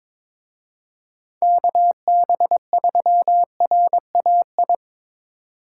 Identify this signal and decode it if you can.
Morse code, heard as a keyed tone, about 24 wpm, 700 Hz. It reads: KB3RAI
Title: Speed Racing - USA Call Signs 22wpm